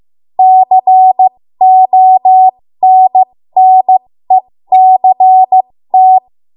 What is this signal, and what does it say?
Morse code, heard as a keyed tone, about 15 wpm, spaced as for 13 wpm, 750 Hz. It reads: CONNECT
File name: Morse / 13WPM